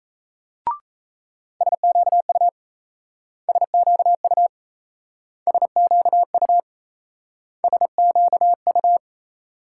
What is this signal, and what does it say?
Morse code, heard as a keyed tone, about 38 wpm, 700 Hz. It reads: SQU SQU SQU SQU